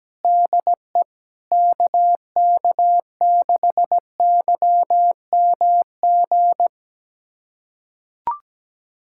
Morse code, heard as a keyed tone, about 17 wpm, 700 Hz. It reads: DE KK6YMG
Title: USA Callsign DE Callsign 17wpm